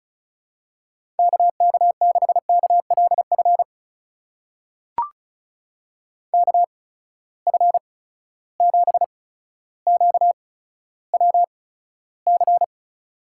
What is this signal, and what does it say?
Morse code, heard as a keyed tone, about 36 wpm, spaced as for 15 wpm, 700 Hz. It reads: KK6KLF KF7QWC
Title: USA Call Signs - 6 Characters - 8x Character Spacing - 35wpm